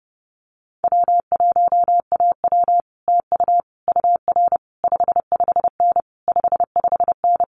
Morse code, heard as a keyed tone, about 30 wpm, 700 Hz. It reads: W1AW TU UR 55N 55N
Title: Common QSO Phrases Encore 30wpm